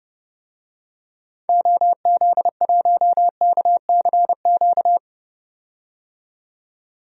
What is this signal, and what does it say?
Morse code, heard as a keyed tone, about 30 wpm, 700 Hz. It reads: OZ1KCQ
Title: DXCC-100 Call Signs - 6 Characters - 30wpm